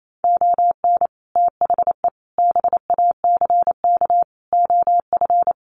Morse code, heard as a keyed tone, about 28 wpm, 700 Hz. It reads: ON THE BACK OF
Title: Sets of 4 Words 28wpm